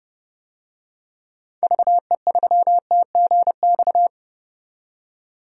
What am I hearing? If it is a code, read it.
Morse code, heard as a keyed tone, about 32 wpm, 700 Hz. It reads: VE3TGX
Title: DXCC-100 Call Signs - 6 Characters - 30wpm